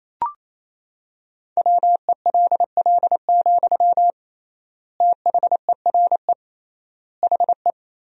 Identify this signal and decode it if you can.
Morse code, heard as a keyed tone, about 28 wpm, 700 Hz. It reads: WELL, THERE HE
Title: 3x Word-Spacing - Sentences from Top 500 Words - 28wpm